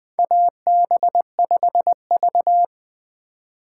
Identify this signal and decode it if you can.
Morse code, heard as a keyed tone, about 21 wpm, 700 Hz. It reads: AB5V
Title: USA Call Signs - 4 Characters - 3x Character Spacing - 20wpm